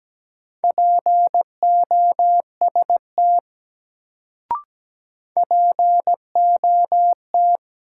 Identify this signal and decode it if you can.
Morse code, heard as a keyed tone, about 17 wpm, 700 Hz. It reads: POST POT
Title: Sentences from Top 1000 Words - New Word Drill - 17wpm